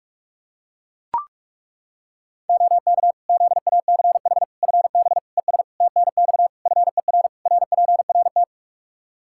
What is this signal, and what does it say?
Morse code, heard as a keyed tone, about 41 wpm, 700 Hz. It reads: OK ZACH FB ES TNX FER RPRT
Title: K1USN SST QSO Practice 45wpm